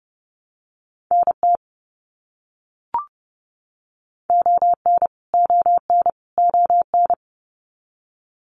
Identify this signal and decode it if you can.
Morse code, heard as a keyed tone, about 30 wpm, 700 Hz. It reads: NT ON ON ON
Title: Canadian Provinces and Territories - Repeated 3x - 30wpm